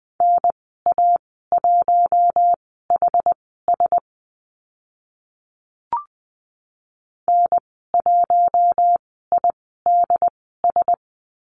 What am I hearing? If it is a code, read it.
Morse code, heard as a keyed tone, about 20 wpm, spaced as for 14 wpm, 700 Hz. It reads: NA1HS N1IDS
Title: 076 Lesson 13g - Letter L - Call Signs 20wpm